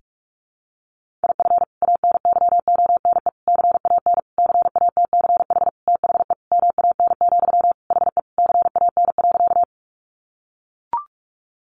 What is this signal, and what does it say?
Morse code, heard as a keyed tone, about 44 wpm, 700 Hz. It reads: IF ANYONE CAN CATCH THE MAN, HE CAN.
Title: Sentences from Top 1500 Words 45wpm